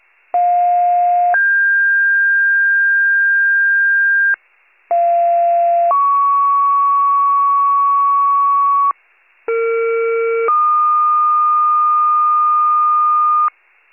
Two-tone page, heard as a one-second tone, then a three-second tone, about 710 then 1600 Hz, about 690 then 1120 Hz, about 460 then 1220 Hz. It is heard from 0.3 to 13.5 s.